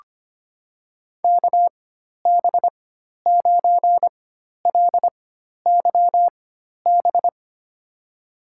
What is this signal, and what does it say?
Morse code, heard as a keyed tone, about 25 wpm, spaced as for 12 wpm, 700 Hz. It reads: KB9LYB
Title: USA Call Signs - 6 Characters - 4x Character Spacing - 25wpm